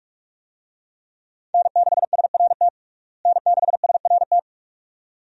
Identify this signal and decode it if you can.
Morse code, heard as a keyed tone, about 46 wpm, 700 Hz. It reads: N6SRT N6SRT
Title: USA Call Signs - 5 Characters - Repeated 2x - 45wpm